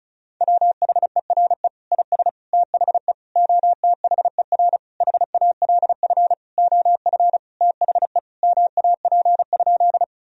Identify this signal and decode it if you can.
Morse code, heard as a keyed tone, about 36 wpm, 700 Hz. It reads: WHERE IS THE OTHER HALF OF THE MAP?
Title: Sentences from Top 1000 Words - Review Part 2 - 35wpm